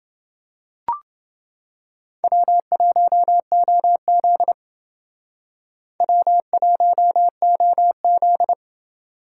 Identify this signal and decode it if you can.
Morse code, heard as a keyed tone, about 28 wpm, 700 Hz. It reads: W1OZ W1OZ